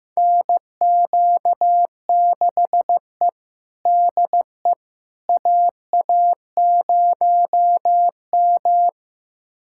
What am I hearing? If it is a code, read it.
Morse code, heard as a keyed tone, about 15 wpm, 700 Hz. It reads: NQ6E DE AA0M
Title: USA Callsign DE Callsign - 4 Characters - 15wpm